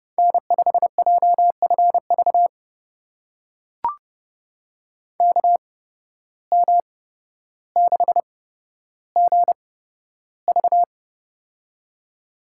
Morse code, heard as a keyed tone, about 30 wpm, spaced as for 13 wpm, 700 Hz. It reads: N5JFV KM6GV
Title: USA Call Signs - 5 Characters - 8x Character Spacing - 30wpm